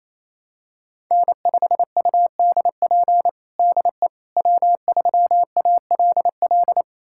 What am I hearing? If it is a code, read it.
Morse code, heard as a keyed tone, about 28 wpm, 700 Hz. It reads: N5UDP DE W3ALL
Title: USA Callsign DE Callsign - 5 Characters - 28wpm